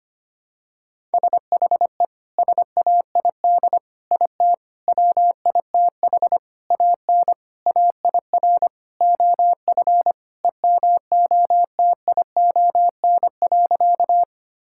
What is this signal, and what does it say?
Morse code, heard as a keyed tone, about 25 wpm, 700 Hz. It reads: SHE SAID IT WITH AN AIR OF EMOTION.
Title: Sentences from Top 1600 Words 25wpm